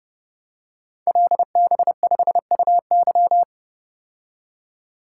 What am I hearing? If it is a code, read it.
Morse code, heard as a keyed tone, about 29 wpm, 700 Hz. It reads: LB5UY
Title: Code Groups - 5 Letter:Numbers - 30wpm